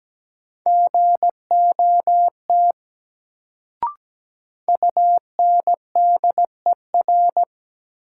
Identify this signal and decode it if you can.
Morse code, heard as a keyed tone, about 17 wpm, 700 Hz. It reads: GOT UNDER
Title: Sentences from Top 300 Words - New Word Drill - 17wpm